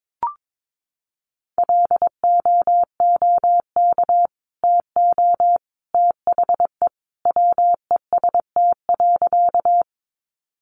Morse code, heard as a keyed tone, about 22 wpm, 700 Hz. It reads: LOOK TO THE WEST.
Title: Sentences from Top 400 Words 22wpm